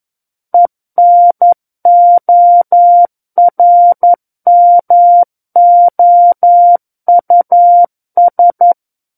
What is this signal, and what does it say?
Morse code, heard as a keyed tone, about 11 wpm, 700 Hz. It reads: ENORMOUS
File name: Book One / 02-Chapter 2 - 11 WPM